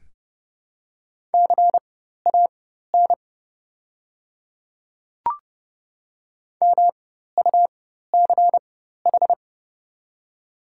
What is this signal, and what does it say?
Morse code, heard as a keyed tone, about 30 wpm, spaced as for 14 wpm, 700 Hz. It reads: CAN MUCH
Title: 4x Character Spacing - Top 300 Words - 30wpm